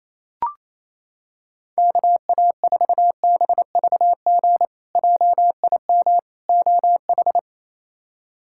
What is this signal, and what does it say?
Morse code, heard as a keyed tone, about 28 wpm, 700 Hz. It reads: KA4BVG JIM OH
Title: K1USN SST QSO Practice - 28wpm